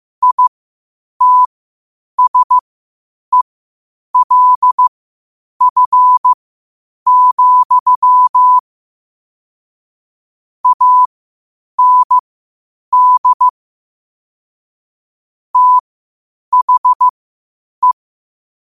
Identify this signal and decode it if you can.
Morse code, heard as a keyed tone, about 15 wpm, spaced as for 8 wpm, 1000 Hz. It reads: ITSELF, AND THE
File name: qotdfiles / 8wpm